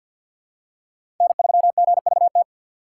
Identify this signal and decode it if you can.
Morse code, heard as a keyed tone, about 52 wpm, 700 Hz. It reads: N3GVT
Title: USA Call Signs - 5 Characters - 6x Character Spacing - 50wpm